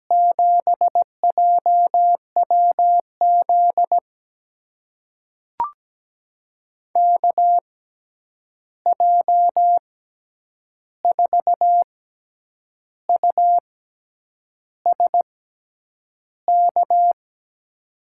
Morse code, heard as a keyed tone, about 17 wpm, spaced as for 8 wpm, 700 Hz. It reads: 7JWZ KJ4USK
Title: USA Call Signs - 6 Characters - 6x Character Spacing - 17wpm